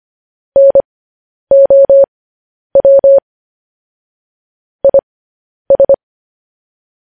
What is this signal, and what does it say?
Morse code, heard as a keyed tone, about 25 wpm, spaced as for 10 wpm, 550 Hz. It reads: NOW IS
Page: news headlines in morse code at 10 wpm